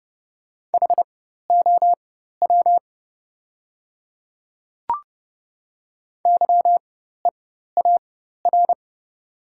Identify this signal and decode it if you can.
Morse code, heard as a keyed tone, about 30 wpm, spaced as for 14 wpm, 700 Hz. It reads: HOW YEAR
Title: 4x Character Spacing - Top 300 Words - 30wpm